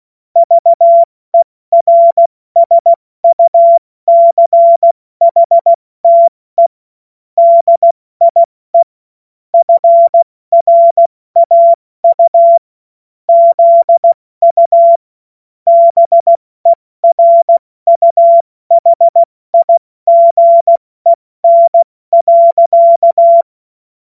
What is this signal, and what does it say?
Morse code, heard as a keyed tone, about 16 wpm, 680 Hz. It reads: VERSUCHTE DIE FRAU ZU BERUHIGEN.
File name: Morsekurs / Witze / Witz 05 Dezember 2023 70 F80